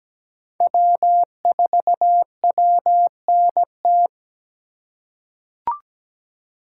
Morse code, heard as a keyed tone, about 17 wpm, 700 Hz. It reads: W4WNT
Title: K1USN SST QSO Practice - 17wpm